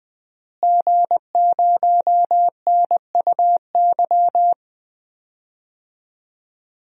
Morse code, heard as a keyed tone, about 21 wpm, 700 Hz. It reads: G0NUY
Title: DXCC-100 Call Signs - 5 Characters - Repeated 2x - 20wpm